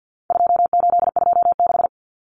Morse code, heard as a keyed tone, about 48 wpm, 700 Hz. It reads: 2826
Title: Numbers - 4-Digit 50wpm